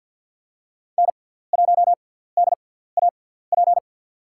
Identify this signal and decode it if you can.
Morse code, heard as a keyed tone, about 52 wpm, spaced as for 17 wpm, 700 Hz. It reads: N1DAP
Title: USA Call Signs - 5 Characters - 6x Character Spacing - 50wpm